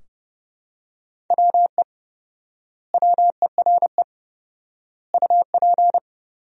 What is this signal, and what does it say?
Morse code, heard as a keyed tone, about 30 wpm, 700 Hz. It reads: WE WERE UP
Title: 4x Word-Spacing - Sentences from Top 400 Words - 30wpm